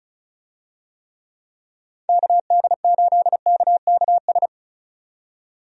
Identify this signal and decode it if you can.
Morse code, heard as a keyed tone, about 34 wpm, 700 Hz. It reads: KD8KKS
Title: USA Call Signs - 6 Characters - 35wpm